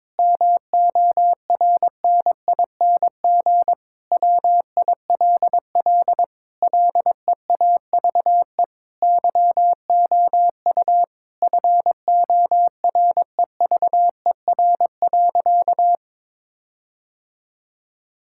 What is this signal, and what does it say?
Morse code, heard as a keyed tone, about 22 wpm, 700 Hz. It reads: MORNING WILL LEAVE YOU FOREVER.
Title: Sentences from Top 1600 Words 22wpm